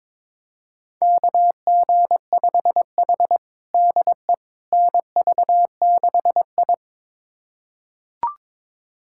Morse code, heard as a keyed tone, about 22 wpm, 700 Hz. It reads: KG5H DE NV6I
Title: USA Callsign DE Callsign - 4 Characters - 22wpm